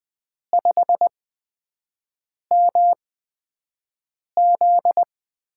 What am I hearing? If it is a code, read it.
Morse code, heard as a keyed tone, about 20 wpm, spaced as for 5 wpm, 700 Hz. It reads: 5MZ
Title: USA Call Signs - 5 Characters - 8x Character Spacing - 20wpm